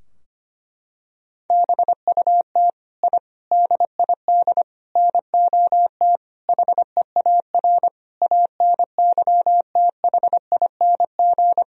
Morse code, heard as a keyed tone, about 25 wpm, 700 Hz. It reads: BUT I DID NOT HEAR ANYTHING
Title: Sentences from Top 1000 Words - Review Part 4 - 25wpm